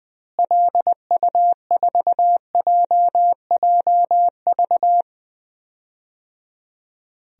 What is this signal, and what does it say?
Morse code, heard as a keyed tone, about 21 wpm, 700 Hz. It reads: LU4JJV
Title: DXCC-100 Call Signs - 6 Characters - Repeated 3x - 20wpm